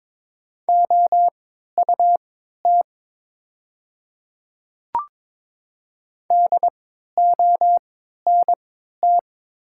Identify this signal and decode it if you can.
Morse code, heard as a keyed tone, about 22 wpm, spaced as for 13 wpm, 700 Hz. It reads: OUT DONT
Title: 3x Character Spacing - Top 300 Words - 22wpm